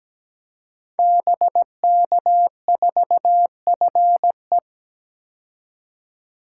Morse code, heard as a keyed tone, about 17 wpm, 700 Hz. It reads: BK4FE
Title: Code Groups - 5 Letter:Numbers - 17wpm